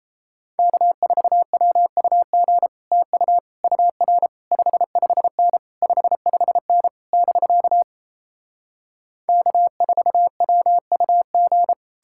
Morse code, heard as a keyed tone, about 37 wpm, 700 Hz. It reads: K4WUG TU UR 55N 55N <BK> K4WUG